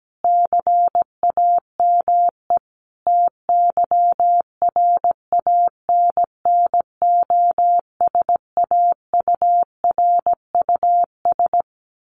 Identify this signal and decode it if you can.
Morse code, heard as a keyed tone, about 17 wpm, 700 Hz. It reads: CAME TYRANNOSAURUS
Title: A Sound Of Thunder 17wpm